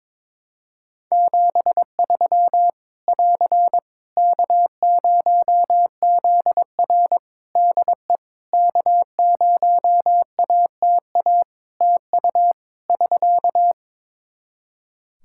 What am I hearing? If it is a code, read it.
Morse code, heard as a keyed tone, about 22 wpm, 700 Hz. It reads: 73 <AR> K0ZR DE K0ATA TU <SK>
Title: Practice QSO - Exchange 4 - Ending - 22wpm